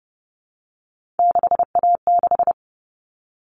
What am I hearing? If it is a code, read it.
Morse code, heard as a keyed tone, about 30 wpm, 700 Hz. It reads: 6A6
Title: All Cut Numbers - 3-Digit 30wpm